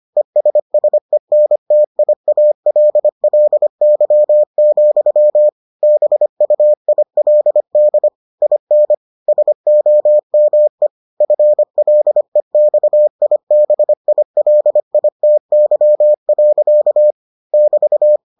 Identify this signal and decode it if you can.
Morse code, heard as a keyed tone, about 25 wpm, 600 Hz. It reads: ESSENTIALLY, BUILD IN SOME FLEXIBILITY. <BT>